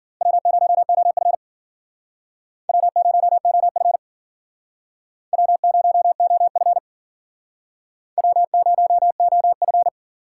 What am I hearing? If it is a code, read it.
Morse code, heard as a keyed tone, about 52 wpm, 700 Hz. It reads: W0OF W0OF W0OF W0OF